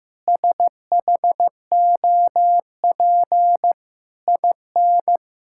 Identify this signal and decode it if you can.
Morse code, heard as a keyed tone, about 14 wpm, 700 Hz. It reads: SHOP IN
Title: Sentences from Top 1600 Words 15wpm